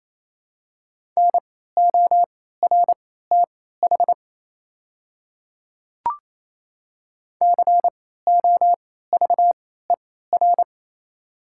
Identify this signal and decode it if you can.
Morse code, heard as a keyed tone, about 28 wpm, spaced as for 16 wpm, 700 Hz. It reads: NORTH COVER
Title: 3x Character Spacing - Top 500 Words - 28wpm